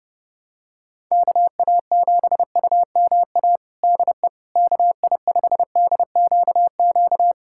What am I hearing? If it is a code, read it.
Morse code, heard as a keyed tone, about 30 wpm, 700 Hz. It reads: KA7UMA DE KI5DQQ
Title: USA Callsign DE Callsign - 6 Characters - 30wpm